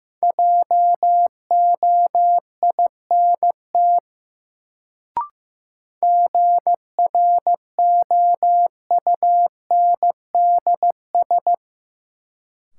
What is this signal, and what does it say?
Morse code, heard as a keyed tone, about 15 wpm, 700 Hz. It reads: JOINT GROUNDS
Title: Sentences from Top 1300 Words - New Word Drill Alt - 15wpm